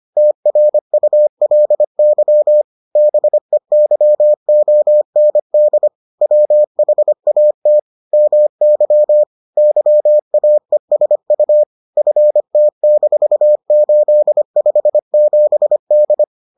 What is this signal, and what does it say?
Morse code, heard as a keyed tone, about 25 wpm, 600 Hz. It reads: TRULY BEYOND WHAT MY YAESU FT-857D